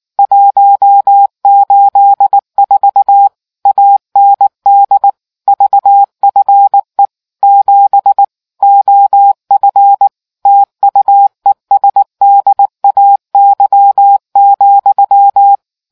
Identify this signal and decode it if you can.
Morse code, heard as a keyed tone, about 19 wpm, 800 Hz. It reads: 184 AND VFE 7 OF TUESDAY,